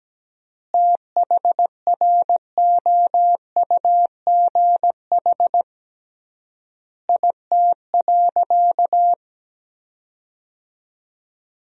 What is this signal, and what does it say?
Morse code, heard as a keyed tone, about 17 wpm, 700 Hz. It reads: THROUGH IT.
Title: 3x Word-Spacing - Sentences from Top 200 Words - 17wpm